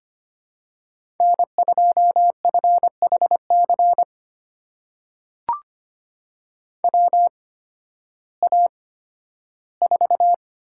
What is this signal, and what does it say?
Morse code, heard as a keyed tone, about 25 wpm, spaced as for 13 wpm, 700 Hz. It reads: N2FHC WA4
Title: USA Call Signs - 5 Characters - 8x Character Spacing - 25wpm